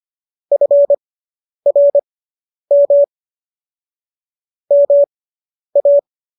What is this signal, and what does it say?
Morse code, heard as a keyed tone, about 25 wpm, spaced as for 10 wpm, 570 Hz. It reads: FRM MA